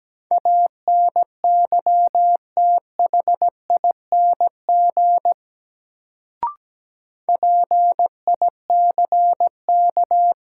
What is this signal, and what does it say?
Morse code, heard as a keyed tone, about 17 wpm, 700 Hz. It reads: ANYTHING PICK
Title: Sentences from Top 1000 Words - New Word Drill - 17wpm